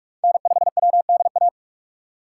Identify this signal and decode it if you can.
Morse code, heard as a keyed tone, about 43 wpm, 690 Hz. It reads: N5WDA
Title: USA Call Signs - 5 Characters - 2x Character Spacing - 45wpm